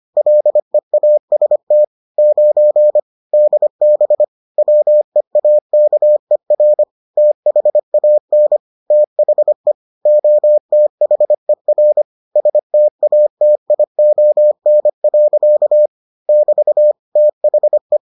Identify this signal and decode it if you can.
Morse code, heard as a keyed tone, about 25 wpm, 600 Hz. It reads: LEAST 9 DB WEAKER THAN THE OTHER STATION. <BT> THE